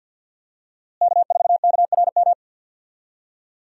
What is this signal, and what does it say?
Morse code, heard as a keyed tone, about 52 wpm, 700 Hz. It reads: K4KRM
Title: USA Call Signs - 5 Characters - Repeated 3x - 50wpm